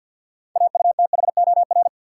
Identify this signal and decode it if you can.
Morse code, heard as a keyed tone, about 51 wpm, 700 Hz. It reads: AUTHOR
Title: Sets of 4 Words - New Word Drill - 50wpm